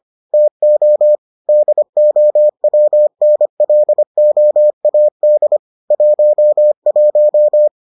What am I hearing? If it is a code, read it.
Morse code, heard as a keyed tone, about 25 wpm, 600 Hz. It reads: TO DOWNLOAD 11